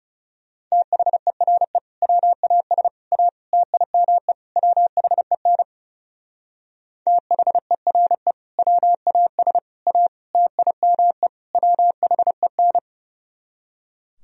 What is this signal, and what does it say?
Morse code, heard as a keyed tone, about 34 wpm, 700 Hz. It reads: THERE WAS A TIME WHEN THERE WAS A TIME WHEN
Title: Speed-Racing - Sets of 5 Words 30wpm